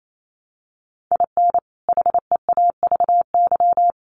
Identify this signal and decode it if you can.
Morse code, heard as a keyed tone, about 28 wpm, 700 Hz. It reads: IN HEAVY
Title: Sentences from Top 1100 Words 28wpm